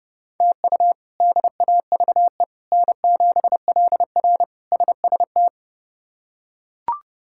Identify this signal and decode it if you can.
Morse code, heard as a keyed tone, about 30 wpm, 700 Hz. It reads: TU DAVE N7LR SST
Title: K1USN SST QSO Practice - 30wpm